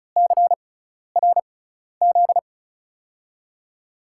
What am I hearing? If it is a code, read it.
Morse code, heard as a keyed tone, about 34 wpm, spaced as for 12 wpm, 700 Hz. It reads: CRZ
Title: USA Call Signs - 6 Characters - 6x Character Spacing - 35wpm